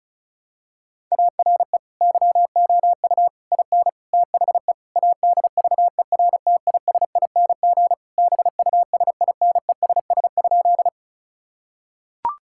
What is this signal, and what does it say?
Morse code, heard as a keyed tone, about 36 wpm, 700 Hz. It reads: ARE YOU IN THE ADVERTISING BUSINESS?
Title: Sentences from Top 1000 Words - Review Part 6 - 35wpm